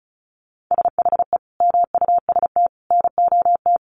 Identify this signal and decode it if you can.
Morse code, heard as a keyed tone, about 34 wpm, 700 Hz. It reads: SHE MUST NOT
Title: Sentences from Top 1500 Words 35wpm